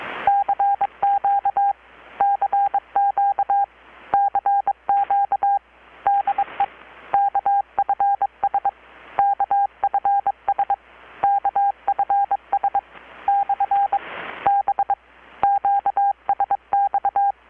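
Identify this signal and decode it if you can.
Morse code, heard as a keyed tone, about 22 wpm, 800 Hz. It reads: CQ CQ CQ DE KFS KFS KFS / B QSX
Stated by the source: Morse from KPH's Night of Nights XX maritime radiotelegraphy event, received in Chiapas, Mexico about 3,800 km away on a Kenwood R-600 with a grounded 10 m long-wire antenna.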